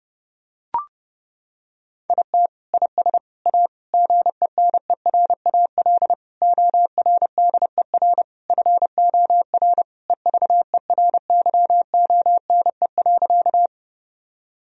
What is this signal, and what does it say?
Morse code, heard as a keyed tone, about 30 wpm, 700 Hz. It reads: IT IS A GENERAL ORDER FOR EVERYONE.
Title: Sentences from Top 1000 Words - Review Part 3 - 30wpm